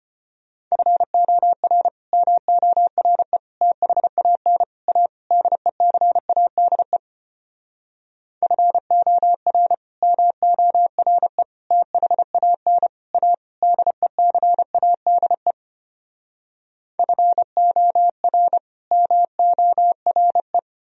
Morse code, heard as a keyed tone, about 33 wpm, 700 Hz. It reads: FOR MORE THAN A DECADE FOR MORE THAN A DECADE FOR MORE